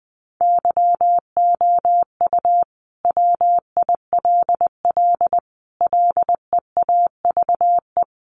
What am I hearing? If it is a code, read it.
Morse code, heard as a keyed tone, about 20 wpm, 700 Hz. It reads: YOU WILL LEAVE
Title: Sentences from Top 900 Words 20wpm